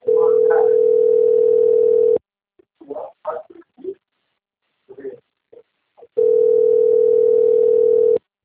Ringback tone (North America), 440 + 480 Hz.